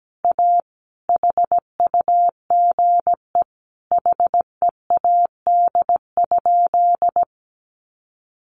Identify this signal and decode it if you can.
Morse code, heard as a keyed tone, about 17 wpm, 700 Hz. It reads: A HUGE HEAD?
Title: Sentences from Top 1100 Words 17wpm